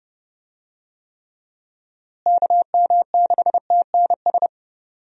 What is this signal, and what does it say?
Morse code, heard as a keyed tone, about 31 wpm, 700 Hz. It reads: KM6TNS
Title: USA Call Signs - 6 Characters - Repeated 2x - 30wpm